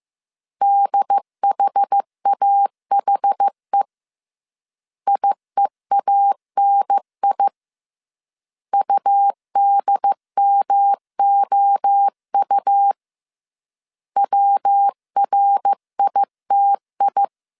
Morse code, heard as a keyed tone, about 15 wpm, 800 Hz. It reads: DHAHE IEANI UDMOU WRITI
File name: CW / DET / 1 3 1500C S C 14wpm 2458